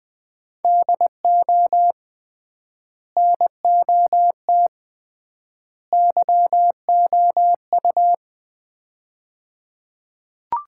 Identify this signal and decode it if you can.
Morse code, heard as a keyed tone, about 20 wpm, 700 Hz. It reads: DO NOT YOU
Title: 3x Word-Spacing - Sets of 3 Words - 20wpm